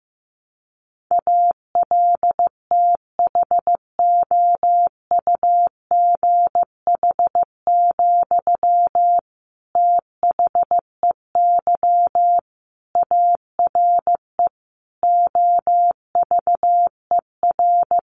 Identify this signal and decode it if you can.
Morse code, heard as a keyed tone, about 15 wpm, 700 Hz. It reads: ALTHOUGH, THEY ARE OVER
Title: Sentences from Top 600 Words 15wpm